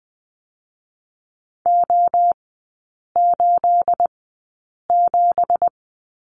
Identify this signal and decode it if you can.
Morse code, heard as a keyed tone, about 20 wpm, 700 Hz. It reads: O 8 7
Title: 218 Lesson 35e - Number 0 - Three Characters 20wpm